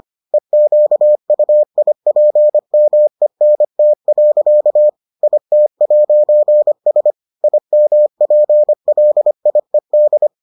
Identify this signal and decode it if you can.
Morse code, heard as a keyed tone, about 25 wpm, 600 Hz. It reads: EQUIPMENT. IT'S IMPLIED